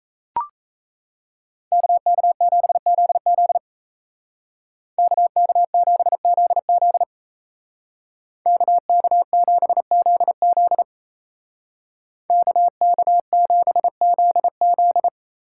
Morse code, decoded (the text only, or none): KK7ZZ KK7ZZ KK7ZZ KK7ZZ